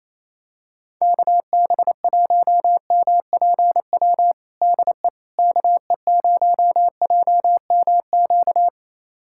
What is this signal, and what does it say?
Morse code, heard as a keyed tone, about 28 wpm, 700 Hz. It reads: KB1MPW DE KE0JMQ
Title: USA Callsign DE Callsign - 6 Characters - 28wpm